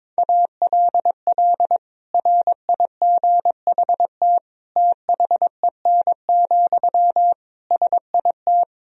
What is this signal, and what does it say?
Morse code, heard as a keyed tone, about 21 wpm, 700 Hz. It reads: ALL RIGHT THEN, SIT
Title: Sentences from Top 500 Words and Abbreviations 22wpm